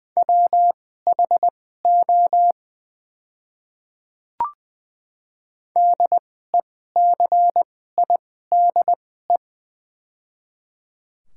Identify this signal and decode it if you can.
Morse code, heard as a keyed tone, about 20 wpm, spaced as for 15 wpm, 700 Hz. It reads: WHO DECIDE
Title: 2x Character Spacing - Top 500 Words - 20wpm